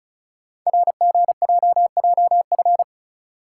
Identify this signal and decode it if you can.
Morse code, heard as a keyed tone, about 36 wpm, 700 Hz. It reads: RGJJF
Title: Code Groups - 5 Letters - 35wpm